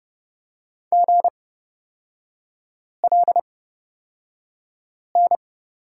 Morse code, heard as a keyed tone, about 30 wpm, spaced as for 5 wpm, 700 Hz. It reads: GLN
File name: Lesson 36e - BT - Three Characters 30wpm